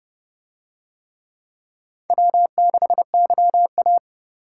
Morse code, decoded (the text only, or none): W6YA